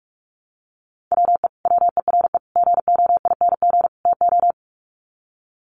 Morse code, heard as a keyed tone, about 44 wpm, 700 Hz. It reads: WE WERE GOING TO
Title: Sets of 4 Words 45wpm